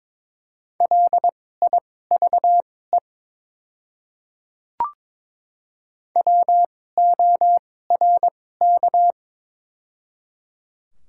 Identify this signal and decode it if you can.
Morse code, heard as a keyed tone, about 22 wpm, spaced as for 16 wpm, 700 Hz. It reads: LIVE WORK
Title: 2x Character Spacing - Top 500 Words - 22wpm